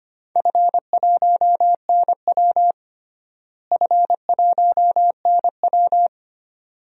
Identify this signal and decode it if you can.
Morse code, heard as a keyed tone, about 25 wpm, 700 Hz. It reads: F1NW F1NW
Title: DXCC-100 Call Signs - 4 Characters - Repeated 2x - 25wpm